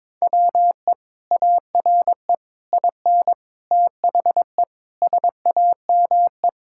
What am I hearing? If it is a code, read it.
Morse code, heard as a keyed tone, about 22 wpm, 700 Hz. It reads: WE ARE IN THE SAME
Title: Sentences from Top 1000 Words - Review Part 6 - 22wpm